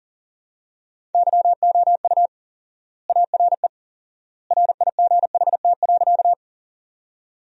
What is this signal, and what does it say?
Morse code, heard as a keyed tone, about 40 wpm, 700 Hz. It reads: YOU ARE RIGHT.
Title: 4x Word-Spacing - Sentences from Top 200 Words - 40wpm